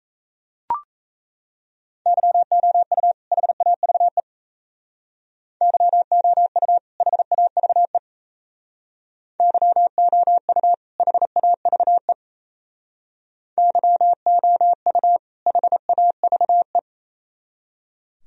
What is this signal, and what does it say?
Morse code, heard as a keyed tone, about 35 wpm, 700 Hz. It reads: YOU HAVE YOU HAVE YOU HAVE YOU HAVE
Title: Speed-Racing - Sets of 2 Words 28wpm